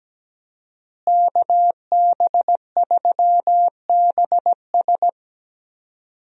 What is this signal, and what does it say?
Morse code, heard as a keyed tone, about 17 wpm, 700 Hz. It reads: KB3BS
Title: USA Call Signs - 5 Characters - Repeated 3x - 17wpm